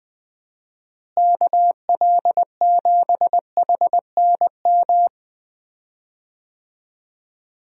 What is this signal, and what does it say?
Morse code, heard as a keyed tone, about 20 wpm, 700 Hz. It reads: KL7HNM
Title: USA Call Signs - 6 Characters - Repeated 3x - 20wpm